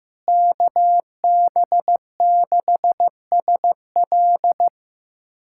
Morse code, heard as a keyed tone, about 15 wpm, 700 Hz. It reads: KB6SL
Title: USA Callsign DE Callsign 15wpm